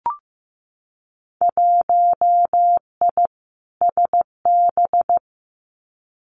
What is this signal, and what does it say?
Morse code, heard as a keyed tone, about 15 wpm, 700 Hz. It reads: 1I SB
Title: Winter Field Day Exchanges Encore 15wpm